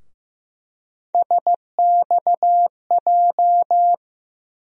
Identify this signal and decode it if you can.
Morse code, heard as a keyed tone, about 15 wpm, 700 Hz. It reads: SXJ
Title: USA Call Sign Postfixes - 3 Letters 15wpm